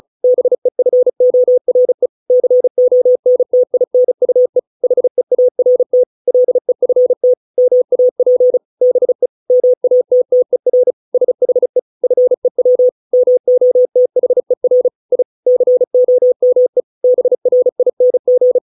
Morse code, heard as a keyed tone, about 36 wpm, 500 Hz. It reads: BEFORE CONTINUE HEART LEFT MAP BE MATTER SHE FEW MOTHER I COME BRING